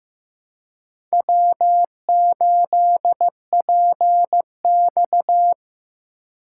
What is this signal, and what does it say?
Morse code, heard as a keyed tone, about 15 wpm, 700 Hz. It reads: W8PX
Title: USA Call Signs - 4 Characters - 2x Character Spacing - 15wpm